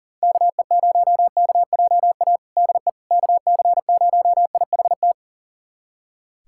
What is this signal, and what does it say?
Morse code, heard as a keyed tone, about 40 wpm, 700 Hz. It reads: KE0KJA DE KC0IHT
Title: USA Callsign DE Callsign - 6 Characters - 40wpm